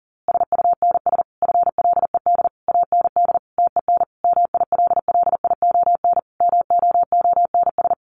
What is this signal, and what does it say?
Morse code, heard as a keyed tone, about 37 wpm, 700 Hz. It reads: SUNS FLED AND TEN MILLION MOONS
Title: A Sound Of Thunder 40wpm